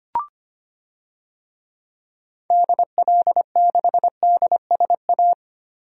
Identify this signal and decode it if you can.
Morse code, heard as a keyed tone, about 25 wpm, 700 Hz. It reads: DL6DSA
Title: DXCC-100 Call Signs - 6 Characters - Repeated 2x - 25wpm